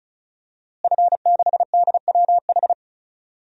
Morse code, heard as a keyed tone, about 34 wpm, 700 Hz. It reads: F6DWH
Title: DXCC-100 Call Signs - 5 Characters - Repeated 2x - 35wpm